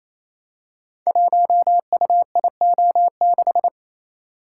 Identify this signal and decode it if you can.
Morse code, heard as a keyed tone, about 27 wpm, 700 Hz. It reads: 1UIO6
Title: Code Groups - 5 Letter:Numbers - 28wpm